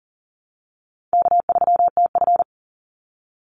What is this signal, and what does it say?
Morse code, heard as a keyed tone, about 40 wpm, 700 Hz. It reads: K3TF
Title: Easy Call Signs Repeated 3x - 40wpm